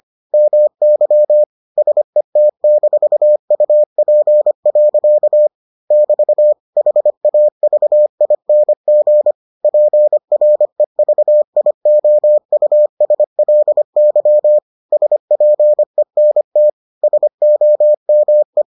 Morse code, heard as a keyed tone, about 25 wpm, 600 Hz. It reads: MY SET-UP. <BT> HAVING PREVIOUSLY SPENT SOME